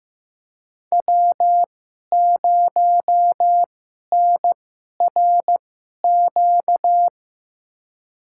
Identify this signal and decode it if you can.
Morse code, heard as a keyed tone, about 15 wpm, spaced as for 11 wpm, 700 Hz. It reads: W0NRQ
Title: USA Call Signs - 5 Characters - 2x Character Spacing - 15wpm